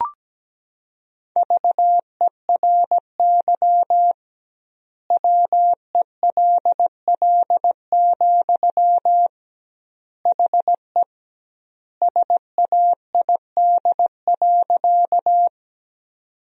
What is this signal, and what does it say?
Morse code, heard as a keyed tone, about 17 wpm, 700 Hz. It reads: VERY WELL, HE SAID.
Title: 2x Word-Spacing - Sentences from Top 200 Words - 17wpm